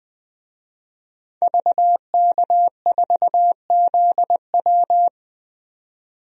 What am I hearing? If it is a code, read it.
Morse code, heard as a keyed tone, about 20 wpm, 700 Hz. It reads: VK4ZW
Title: DXCC-100 Call Signs - 5 Characters - 20wpm